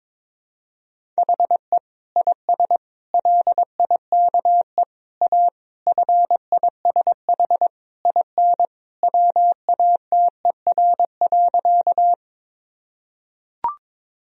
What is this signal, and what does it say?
Morse code, heard as a keyed tone, about 22 wpm, 700 Hz. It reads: HE IS LIKE A FISH IN WATER.
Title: Sentences from Top 1000 Words - Review Part 1 - 22wpm